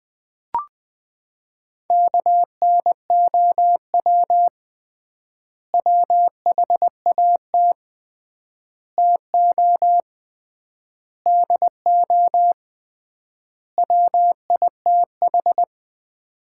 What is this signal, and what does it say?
Morse code, heard as a keyed tone, about 20 wpm, 700 Hz. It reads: KNOW WHAT TO DO WITH
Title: 3x Word-Spacing - Sets of 5 Words - 20wpm